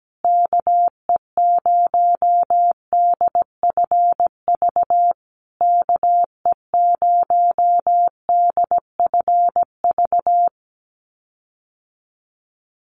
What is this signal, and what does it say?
Morse code, heard as a keyed tone, about 17 wpm, 700 Hz. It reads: KE0DFV KE0DFV
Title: USA Call Signs Repeated 3x 17wpm